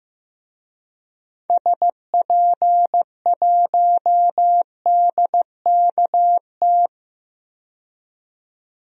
Morse code, heard as a keyed tone, about 15 wpm, 700 Hz. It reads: SP1DKT
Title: DXCC-100 Call Signs - 6 Characters - 15wpm